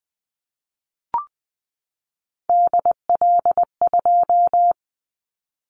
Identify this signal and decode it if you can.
Morse code, heard as a keyed tone, about 20 wpm, 700 Hz. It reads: DL2
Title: Callsign Prefixes for Top 25 DXCC - 20wpm